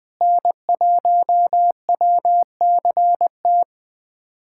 Morse code, heard as a keyed tone, about 20 wpm, 700 Hz. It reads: N1WCT
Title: USA Callsign DE Callsign - 5 Characters - 20wpm